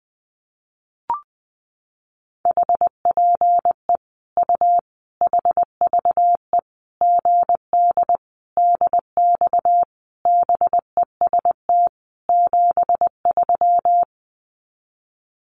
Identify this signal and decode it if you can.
Morse code, heard as a keyed tone, about 19 wpm, 700 Hz. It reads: HPE U HVE GD DX BEST 73
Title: Common QSO Phrases 20wpm